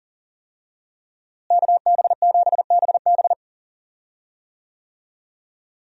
Morse code, heard as a keyed tone, about 40 wpm, 700 Hz. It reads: KB7BB
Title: USA Call Signs - 5 Characters - Repeated 2x - 40wpm